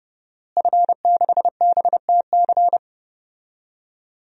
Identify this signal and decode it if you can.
Morse code, heard as a keyed tone, about 30 wpm, 700 Hz. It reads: F6BTC
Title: DXCC-100 Call Signs - 5 Characters - Repeated 2x - 30wpm